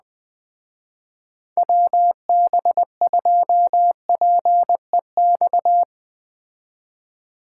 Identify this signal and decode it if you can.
Morse code, heard as a keyed tone, about 20 wpm, 700 Hz. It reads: WB2PEX
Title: Speed Racing - USA 2x3 Call Signs 17wpm